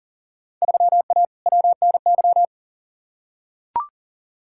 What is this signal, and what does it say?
Morse code, heard as a keyed tone, about 40 wpm, 700 Hz. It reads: 3A WNY
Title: Field Day Exchanges - 40wpm